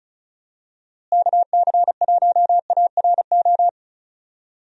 Morse code, heard as a keyed tone, about 36 wpm, 700 Hz. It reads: KC1ARO
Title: USA Call Signs - 6 Characters - Repeated 3x - 35wpm